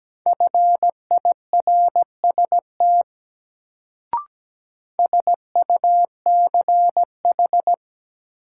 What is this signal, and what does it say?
Morse code, heard as a keyed tone, about 17 wpm, 700 Hz. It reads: FIRST SUCH
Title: Sets of 2 Words - New Word Drill - 17wpm